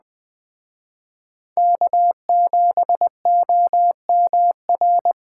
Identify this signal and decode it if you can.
Morse code, heard as a keyed tone, about 21 wpm, 700 Hz. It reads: K7OMR